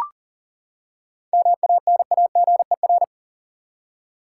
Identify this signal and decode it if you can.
Morse code, heard as a keyed tone, about 40 wpm, 700 Hz. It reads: MANAGER
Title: Seven Letter Words - 40wpm